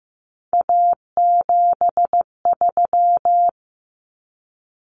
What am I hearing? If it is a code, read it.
Morse code, heard as a keyed tone, about 15 wpm, 700 Hz. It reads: A73
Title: Common Cut Numbers - 3-Digit 15wpm